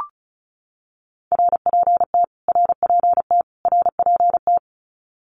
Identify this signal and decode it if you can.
Morse code, heard as a keyed tone, about 34 wpm, 700 Hz. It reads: RPT RPT RPT
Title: Abbreviations Repeated 3x - 35wpm